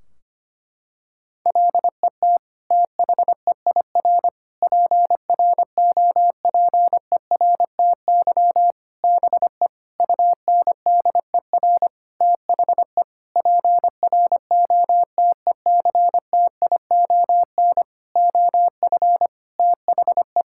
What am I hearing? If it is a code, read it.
Morse code, heard as a keyed tone, about 25 wpm, 700 Hz. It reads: LET THEIR PROPERTY BE UNDER THE PROTECTION OF THE
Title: Sentences from Top 1600 Words 25wpm